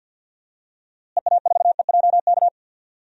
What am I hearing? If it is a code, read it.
Morse code, heard as a keyed tone, about 52 wpm, 700 Hz. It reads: EA4EJK
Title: DXCC-100 Call Signs - 6 Characters - Repeated 2x - 50wpm